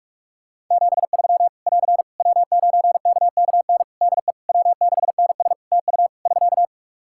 Morse code, heard as a keyed tone, about 46 wpm, 700 Hz. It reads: 73 <AR> W9KKN DE W6NS TU <SK>
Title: Practice QSO - Exchange 4 - Ending - 45wpm